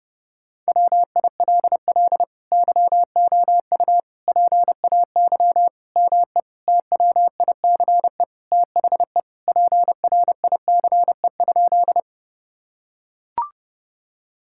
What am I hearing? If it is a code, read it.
Morse code, heard as a keyed tone, about 30 wpm, 700 Hz. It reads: WILL YOU PAY ME TWICE THE PRICE?
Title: Sentences from Top 1600 Words 30wpm